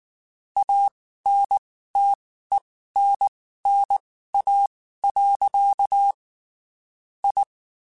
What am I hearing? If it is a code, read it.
Morse code, heard as a keyed tone, about 19 wpm, spaced as for 14 wpm, 800 Hz. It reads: ANTENNA. I
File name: (spaced) / QSOs Spaced-mp3s / kc7sac @18 sp